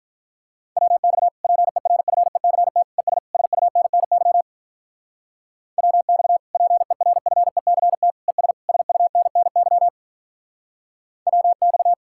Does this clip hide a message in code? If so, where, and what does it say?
Yes: Morse code from 0.8 to 11.9 s, heard as a keyed tone, about 45 wpm, 700 Hz. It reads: WX PERFECT ES SUNNY WX PERFECT ES SUNNY WX